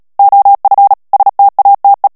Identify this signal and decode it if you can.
Morse code, heard as a keyed tone, about 37 wpm, 800 Hz. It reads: OF STATE
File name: morse mp3 / ANGLAIS texte clair 3 vitesse 35 mots minute